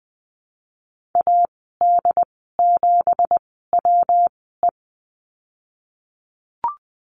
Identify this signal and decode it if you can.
Morse code, heard as a keyed tone, about 20 wpm, spaced as for 15 wpm, 700 Hz. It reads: AD7WE
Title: 167 Lesson 27g - Number 7 - Call Signs 20wpm